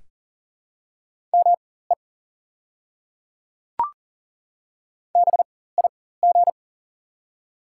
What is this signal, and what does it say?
Morse code, heard as a keyed tone, about 40 wpm, spaced as for 19 wpm, 690 Hz. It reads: ME BIG
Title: 4x Character Spacing - Top 300 Words - 40wpm